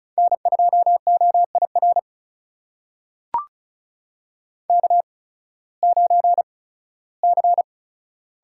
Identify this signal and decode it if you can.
Morse code, heard as a keyed tone, about 36 wpm, spaced as for 19 wpm, 700 Hz. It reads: N2OIR K9C
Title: USA Call Signs - 8x Character Spacing - 35wpm